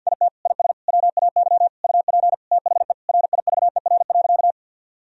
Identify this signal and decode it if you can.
Morse code, heard as a keyed tone, about 52 wpm, 700 Hz. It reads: IT IS WAY UP THE RIVER.
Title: Sentences from Top 1000 Words - Review Part 1 - 50wpm